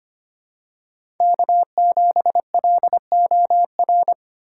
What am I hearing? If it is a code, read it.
Morse code, heard as a keyed tone, about 25 wpm, 700 Hz. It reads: K7LOR
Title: USA Call Signs - 4x Character Spacing - 25wpm